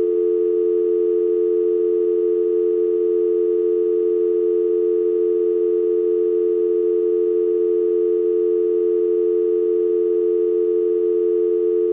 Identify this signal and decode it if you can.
Dial tone (North America), 350 + 440 Hz.
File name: scratches-windows-gamerip-2006 / hqgyzmrpsn / fx-phone tone